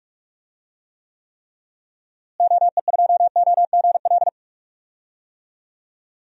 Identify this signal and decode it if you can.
Morse code, heard as a keyed tone, about 46 wpm, 700 Hz. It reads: OE2OGL
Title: DXCC-100 Call Signs - 6 Characters - Repeated 3x - 45wpm